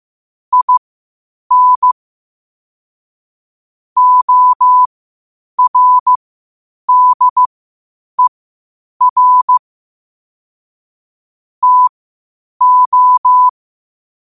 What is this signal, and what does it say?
Morse code, heard as a keyed tone, about 15 wpm, spaced as for 8 wpm, 1000 Hz. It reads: IN ORDER TO